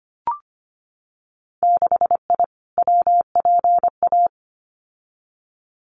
Morse code, heard as a keyed tone, about 25 wpm, 700 Hz. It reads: 6I WPA
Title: Winter Field Day Exchanges Encore 25wpm